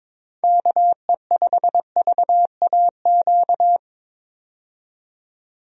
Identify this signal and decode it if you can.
Morse code, heard as a keyed tone, about 23 wpm, 700 Hz. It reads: KE5VAQ
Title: USA Call Signs - 6 Characters - 22wpm